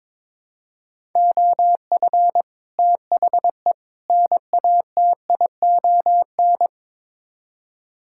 Morse code, heard as a keyed tone, about 22 wpm, 700 Hz. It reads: OF THE NATION
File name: practice / 22wpm / Build A Phrase - 5-Word Phrases - 22wpm